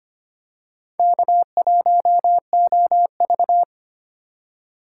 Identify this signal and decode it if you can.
Morse code, heard as a keyed tone, about 25 wpm, 700 Hz. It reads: K1OV
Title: USA Call Signs - 4 Characters - 3x Character Spacing - 25wpm